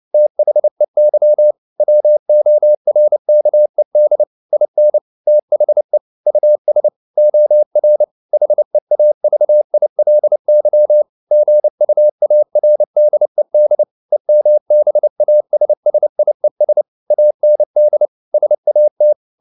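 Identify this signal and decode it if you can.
Morse code, heard as a keyed tone, about 29 wpm, 600 Hz. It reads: THEY WORKED IN THE US OR HEAVILY GUARDED EMBASSIES AND SAT
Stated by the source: Morse code at 29 wpm, sending Ken Follett's novel Never in English, from CW med Gnister.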